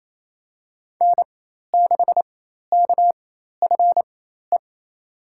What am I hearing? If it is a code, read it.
Morse code, heard as a keyed tone, about 28 wpm, spaced as for 13 wpm, 700 Hz. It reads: N6KFE
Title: USA Call Signs - 5 Characters - 4x Character Spacing - 28wpm